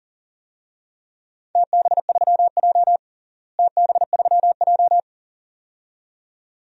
Morse code, heard as a keyed tone, about 38 wpm, 700 Hz. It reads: TB3J TB3J
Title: DXCC-100 Call Signs - 4 Characters - Repeated 2x - 40wpm